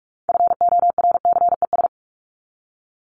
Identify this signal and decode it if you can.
Morse code, heard as a keyed tone, about 44 wpm, 700 Hz. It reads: FORCES
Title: ES:S Suffix - Top 100 Words with Base Words 45wpm